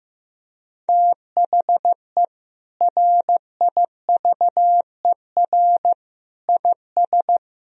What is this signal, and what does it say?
Morse code, heard as a keyed tone, about 15 wpm, 700 Hz. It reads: THE RIVER IS
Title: Sentences from Top 1000 Words - Review Part 1 - 15wpm